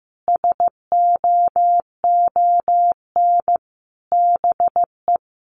Morse code, heard as a keyed tone, about 15 wpm, 700 Hz. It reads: SOON BE
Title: Sentences from Top 1300 Words 15wpm